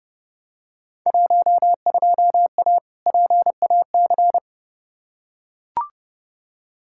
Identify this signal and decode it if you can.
Morse code, heard as a keyed tone, about 32 wpm, 700 Hz. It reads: 12A PAC
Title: Field Day Exchanges Encore - 30wpm